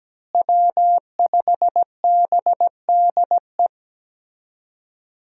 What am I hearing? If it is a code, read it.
Morse code, heard as a keyed tone, about 17 wpm, 700 Hz. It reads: W5BDE
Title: USA Call Signs - 5 Characters - Repeated 2x - 17wpm